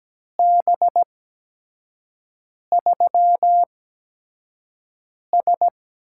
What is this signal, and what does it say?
Morse code, heard as a keyed tone, about 17 wpm, spaced as for 5 wpm, 700 Hz. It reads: B3S
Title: USA Call Signs - 4 Characters - 8x Character Spacing - 17wpm